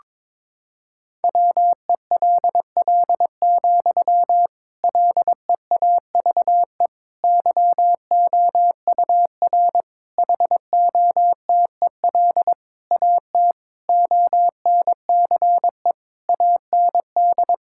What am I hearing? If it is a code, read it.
Morse code, heard as a keyed tone, about 22 wpm, 700 Hz. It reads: WELL, LEAVE YOUR HOTEL AT ONCE AND
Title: Sentences from Top 1600 Words 22wpm